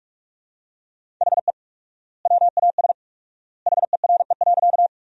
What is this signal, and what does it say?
Morse code, heard as a keyed tone, about 47 wpm, 700 Hz. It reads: HE WAS HERE.
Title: 4x Word-Spacing - Sentences from Top 200 Words - 45wpm